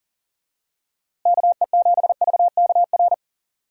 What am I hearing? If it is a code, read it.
Morse code, heard as a keyed tone, about 40 wpm, 700 Hz. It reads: KE7VKR
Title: USA Call Signs - 6 Characters - Repeated 3x - 40wpm